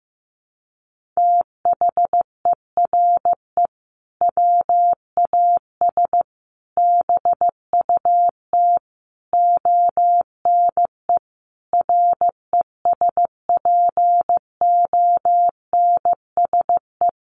Morse code, heard as a keyed tone, about 15 wpm, 700 Hz. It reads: THERE WAS BUT ONE RESPONSE
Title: Sentences from Top 1400 Words 15wpm